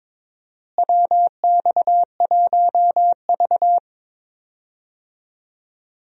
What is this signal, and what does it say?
Morse code, heard as a keyed tone, about 22 wpm, 700 Hz. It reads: WX1V
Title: USA Call Signs - 4 Characters - Repeated 3x - 22wpm